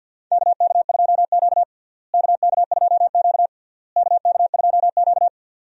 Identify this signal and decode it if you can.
Morse code, heard as a keyed tone, about 52 wpm, 700 Hz. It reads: KK2Q KK2Q KK2Q
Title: USA Call Signs - 4 Characters - Repeated 3x - 50wpm